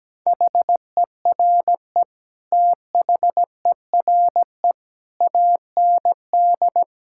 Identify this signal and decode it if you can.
Morse code, heard as a keyed tone, about 17 wpm, 700 Hz. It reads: HERE THERE AND
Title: Sentences from Top 1600 Words 17wpm